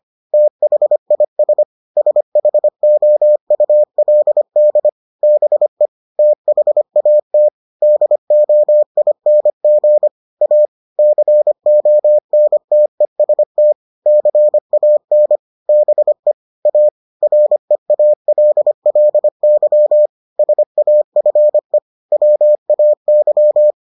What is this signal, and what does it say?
Morse code, heard as a keyed tone, about 25 wpm, 600 Hz. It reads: THIS SHOULD BE THAT DOING A CONTEST CAN BE A REALLY SAFE WAY